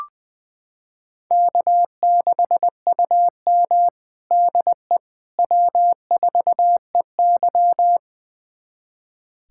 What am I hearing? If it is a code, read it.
Morse code, heard as a keyed tone, about 20 wpm, 700 Hz. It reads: K6UM DE W4EY
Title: USA Callsign DE Callsign - 4 Characters - 20wpm